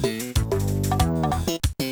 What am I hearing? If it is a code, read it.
There is no code.